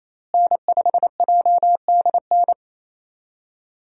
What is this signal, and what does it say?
Morse code, heard as a keyed tone, about 28 wpm, 700 Hz. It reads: N5JDN
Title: USA Call Signs - 5 Characters - Repeated 3x - 28wpm